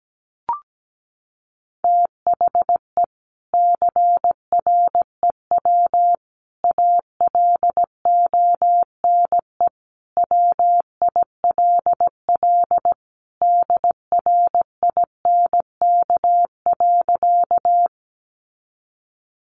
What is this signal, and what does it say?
Morse code, heard as a keyed tone, about 17 wpm, 700 Hz. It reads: THE CREW ALONE WILL DRINK.
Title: Sentences from Top 1500 Words 17wpm